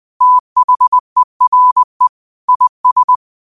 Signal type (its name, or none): Morse code